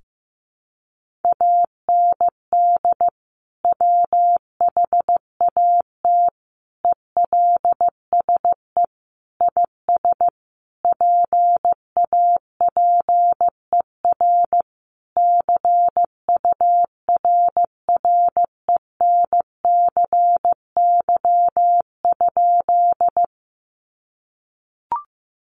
Morse code, heard as a keyed tone, about 15 wpm, 700 Hz. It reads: AND WHAT ELSE IS PAPER CURRENCY?
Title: Sentences from Top 1500 Words 15wpm